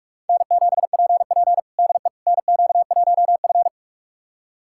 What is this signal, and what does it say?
Morse code, heard as a keyed tone, about 41 wpm, 700 Hz. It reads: N7PP DE NQ1F
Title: USA Callsign DE Callsign - 4 Characters - 45wpm